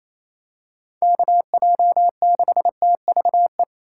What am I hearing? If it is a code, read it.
Morse code, heard as a keyed tone, about 28 wpm, 700 Hz. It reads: KJ6TVE
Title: USA Call Signs - 6 Characters - 4x Character Spacing - 28wpm